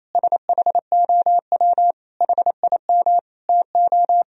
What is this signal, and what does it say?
Morse code, heard as a keyed tone, about 28 wpm, 700 Hz. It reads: SHOW HIM TO
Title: Sentences from Top 200 Words and Abbreviations 28wpm